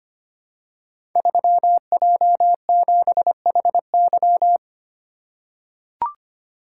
Morse code, heard as a keyed tone, about 25 wpm, 700 Hz. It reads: 3J7HY
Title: Code Groups - 5 Letter:Numbers - 25wpm